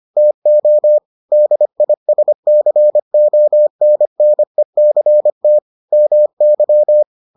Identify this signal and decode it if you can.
Morse code, heard as a keyed tone, about 25 wpm, 600 Hz. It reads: TO DISCONNECT MY